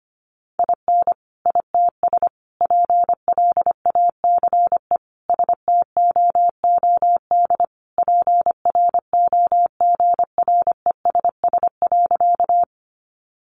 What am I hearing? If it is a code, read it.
Morse code, heard as a keyed tone, about 25 wpm, 700 Hz. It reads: IN ITS PLACE STOOD PROGRESS.
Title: Sentences from Top 1400 Words 25wpm